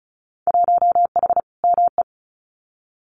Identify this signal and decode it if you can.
Morse code, heard as a keyed tone, about 34 wpm, 700 Hz. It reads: 1H ME
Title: Winter Field Day Exchanges 35wpm